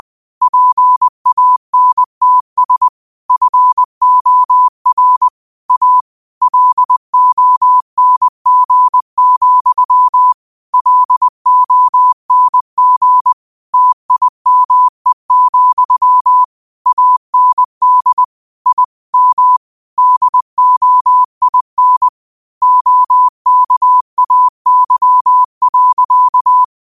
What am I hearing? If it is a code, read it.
Morse code, heard as a keyed tone, about 20 wpm, 1000 Hz. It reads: PANTS FOR A LONG, LONG TIME, AND I M DOIN OKAY.